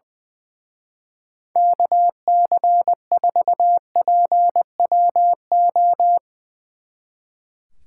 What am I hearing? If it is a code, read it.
Morse code, heard as a keyed tone, about 22 wpm, 700 Hz. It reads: KC4PWO
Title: Speed Racing - USA 2x3 Call Signs 20wpm